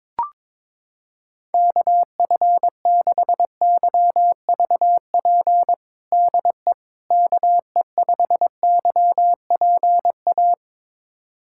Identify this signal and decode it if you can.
Morse code, heard as a keyed tone, about 22 wpm, 700 Hz. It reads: KF6YVP DE KE5YPA
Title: USA Callsign DE Callsign 22wpm